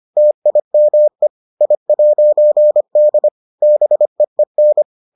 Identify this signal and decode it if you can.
Morse code, heard as a keyed tone, about 25 wpm, 600 Hz. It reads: TIME I'D BEEN